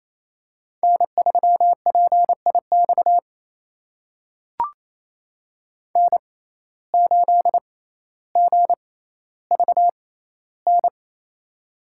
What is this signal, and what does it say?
Morse code, heard as a keyed tone, about 27 wpm, spaced as for 14 wpm, 700 Hz. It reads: N3PIX N8GVN
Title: USA Call Signs - 5 Characters - 6x Character Spacing - 28wpm